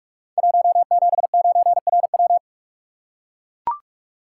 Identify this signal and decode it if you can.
Morse code, heard as a keyed tone, about 44 wpm, 700 Hz. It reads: 179RW
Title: Code Groups - 5 Letter:Numbers - 45wpm